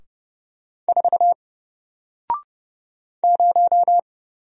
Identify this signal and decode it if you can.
Morse code, heard as a keyed tone, about 31 wpm, 700 Hz. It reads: 4 0
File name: Single-Letter-Number-30wpm